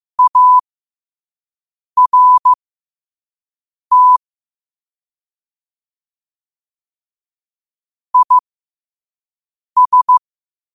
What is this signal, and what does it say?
Morse code, heard as a keyed tone, about 15 wpm, spaced as for 5 wpm, 1000 Hz. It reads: ART IS
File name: qotdfiles / 5wpm